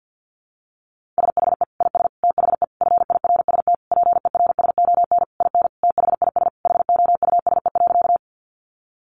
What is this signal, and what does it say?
Morse code, heard as a keyed tone, about 50 wpm, 700 Hz. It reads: SHE IS THE FIRST PERSON IN THIS HOUSE.
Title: Sentences from Top 400 Words 50wpm